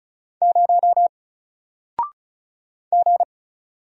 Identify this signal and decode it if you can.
Morse code, heard as a keyed tone, about 36 wpm, 700 Hz. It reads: 0 G
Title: Single Letter-Number Alt 35wpm